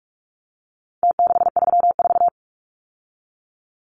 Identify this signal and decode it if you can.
Morse code, heard as a keyed tone, about 44 wpm, 700 Hz. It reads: T634
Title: Numbers - Contest Style - 4-Digit 45wpm